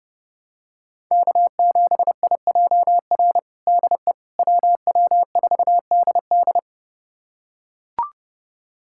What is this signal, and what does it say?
Morse code, heard as a keyed tone, about 30 wpm, 700 Hz. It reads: K7IJR DE WW4DD
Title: USA Callsign DE Callsign - 5 Characters - 30wpm